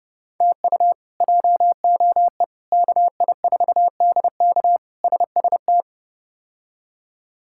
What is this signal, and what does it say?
Morse code, heard as a keyed tone, about 30 wpm, 700 Hz. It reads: TU JOE KI4DK SST
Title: K1USN SST QSO Practice - 30wpm